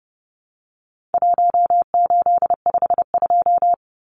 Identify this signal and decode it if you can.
Morse code, heard as a keyed tone, about 30 wpm, 700 Hz. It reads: 1852
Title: Numbers - 4-Digit 30wpm